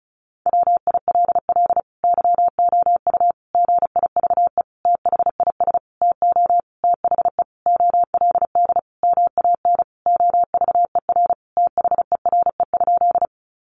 Morse code, heard as a keyed tone, about 34 wpm, 700 Hz. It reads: WILL YOU GIVE THIS TO THE OLD MAN OVER THERE?
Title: Sentences from Top 200 Words 35wpm